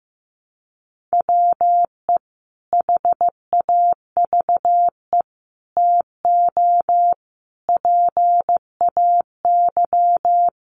Morse code, heard as a keyed tone, about 15 wpm, 700 Hz. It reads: WE HAVE TO PAY
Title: A Sound Of Thunder 15wpm